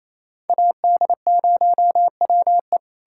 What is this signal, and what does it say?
Morse code, heard as a keyed tone, about 38 wpm, spaced as for 34 wpm, 700 Hz. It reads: AD0WE
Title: Speed-Racing Common QSO Phrases Encore - 28wpm